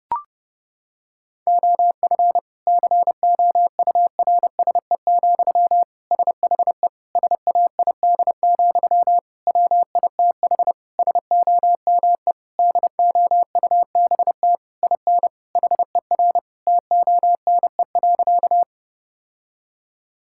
Morse code, heard as a keyed tone, about 32 wpm, 700 Hz. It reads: OF COURSE, SHE SAID, WITH SOME DOUBT IN HER TONE.
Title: Sentences from Top 1600 Words 30wpm